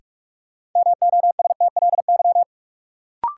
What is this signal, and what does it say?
Morse code, heard as a keyed tone, about 47 wpm, 700 Hz. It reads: MOSTLY
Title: Sentences from Top 1300 Words - New Word Drill Alt - 45wpm